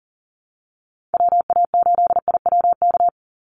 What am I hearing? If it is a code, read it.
Morse code, heard as a keyed tone, about 40 wpm, 700 Hz. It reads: WA8IWK
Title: Call Signs Repeated 2x - 40wpm